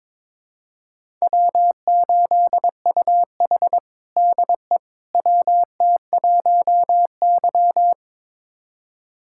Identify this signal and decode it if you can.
Morse code, heard as a keyed tone, about 22 wpm, 700 Hz. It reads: W8UH DE WT1Y
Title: USA Callsign DE Callsign - 4 Characters - 22wpm